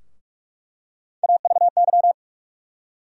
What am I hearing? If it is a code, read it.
Morse code, heard as a keyed tone, about 45 wpm, 700 Hz. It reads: AVY
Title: USA Call Sign Postfixes - 3 Letters 45wpm